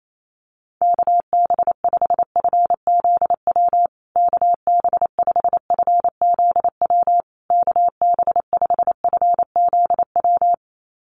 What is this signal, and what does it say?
Morse code, heard as a keyed tone, about 28 wpm, 700 Hz. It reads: KB5FZW KB5FZW KB5FZW
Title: USA Call Signs Repeated 3x 28wpm